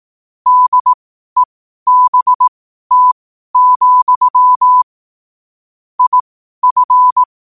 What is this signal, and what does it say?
Morse code, heard as a keyed tone, about 18 wpm, spaced as for 12 wpm, 1000 Hz. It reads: DEBT, IF
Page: New quotes every day in morse code at 12 Words per minute.